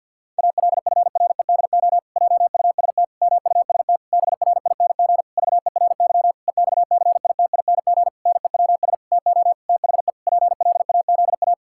Morse code, heard as a keyed tone, about 48 wpm, 700 Hz. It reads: ALFREDO JUST MUST BRING VERY EXCITING NEWS TO THE PLAZA
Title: Pangram Challenge v2 - 50wpm